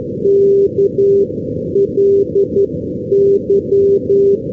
Morse code, heard as a keyed tone, about 11 wpm, spaced as for 9 wpm, 400 Hz. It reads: KLY
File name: europe / irl / irl-330 ndb dublin-killiney kly-378-np-g-oldham-2012-rb k3